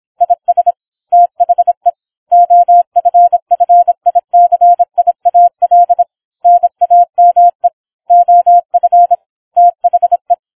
Morse code, heard as a keyed tone, about 26 wpm, 700 Hz. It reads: IS THE OFFICIAL NAME OF THE